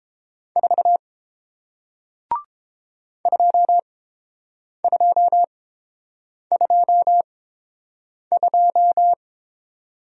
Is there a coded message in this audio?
Yes: Morse code from 0.6 to 9.1 s, heard as a keyed tone, about 30 wpm, spaced as for 7 wpm, 700 Hz. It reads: 4 2222